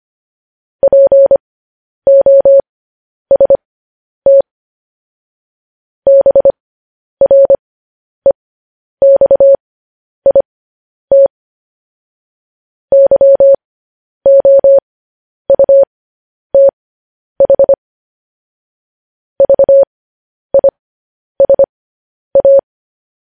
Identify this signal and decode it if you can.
Morse code, heard as a keyed tone, about 25 wpm, spaced as for 10 wpm, 550 Hz. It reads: POST BREXIT YOUTH VISA
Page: news headlines in morse code at 10 wpm